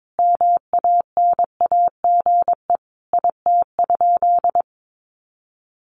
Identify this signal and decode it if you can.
Morse code, heard as a keyed tone, about 22 wpm, 700 Hz. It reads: MANAGE IT?
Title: Sentences from Top 1400 Words 22wpm